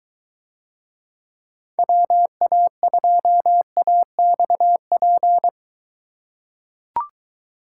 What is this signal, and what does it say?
Morse code, heard as a keyed tone, about 23 wpm, 700 Hz. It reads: WA2AXP